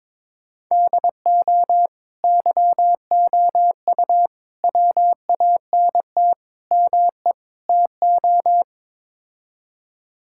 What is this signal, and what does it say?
Morse code, heard as a keyed tone, about 22 wpm, 700 Hz. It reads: DO YOU WANT ME TO
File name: practice / 22wpm / Build A Phrase - 5-Word Phrases - 22wpm